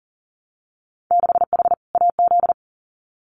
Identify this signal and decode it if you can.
Morse code, heard as a keyed tone, about 40 wpm, 700 Hz. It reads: 6H AZ
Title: Winter Field Day Exchanges Encore 40wpm